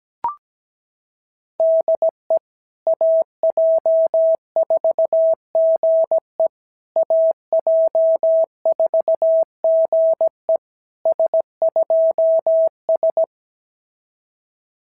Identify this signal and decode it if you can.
Morse code, heard as a keyed tone, about 17 wpm, 690 Hz. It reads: DE AJ4GE AJ4GE S2S
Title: SOTA QSO Practice 17wpm